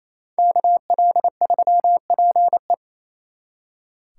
Morse code, heard as a keyed tone, about 36 wpm, 700 Hz. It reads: KL3PE